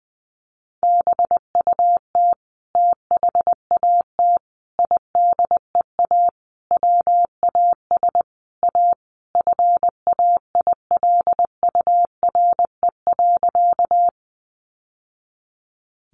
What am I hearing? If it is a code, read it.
Morse code, heard as a keyed tone, about 20 wpm, 700 Hz. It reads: BUT THAT IDEA WAS A FAILURE.
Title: Sentences from Top 900 Words 20wpm